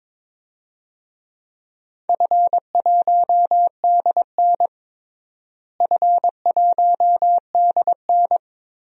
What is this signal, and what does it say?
Morse code, heard as a keyed tone, about 22 wpm, 700 Hz. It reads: F1DN F1DN
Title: DXCC-100 Call Signs - 4 Characters - Repeated 3x - 22wpm